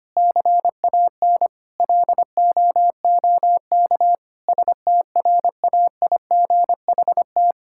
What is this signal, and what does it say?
Morse code, heard as a keyed tone, about 25 wpm, 700 Hz. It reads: CAN LOOK STRAIGHT
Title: Sentences from Top 1000 Words - Review Part 6 - 25wpm